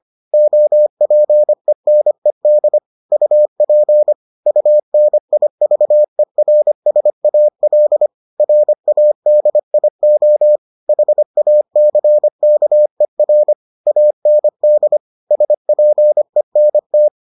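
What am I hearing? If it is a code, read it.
Morse code, heard as a keyed tone, about 25 wpm, 600 Hz. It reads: OPENED UP UNIVERSAL RADIO HACKER AND SPENT
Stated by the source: Morse edition of the Foundations of Amateur Radio episodes.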